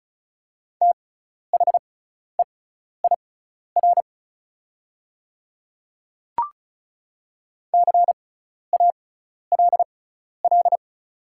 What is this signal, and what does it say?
Morse code, heard as a keyed tone, about 36 wpm, spaced as for 12 wpm, 700 Hz. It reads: THEIR CALL
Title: 6x Character Spacing - Top 100 Words - 35wpm